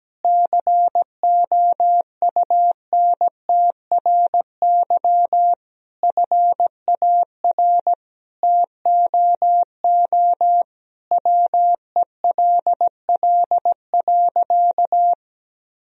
Morse code, heard as a keyed tone, about 17 wpm, 700 Hz. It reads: COUNTRY FAR TOO WELL.
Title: Sentences from Top 1000 Words - Review Part 3 - 17wpm